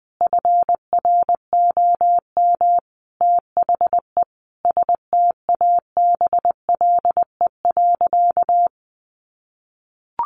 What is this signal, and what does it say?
Morse code, heard as a keyed tone, about 20 wpm, 700 Hz. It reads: FROM THE STABLE.
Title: Sentences from Top 1500 Words 20wpm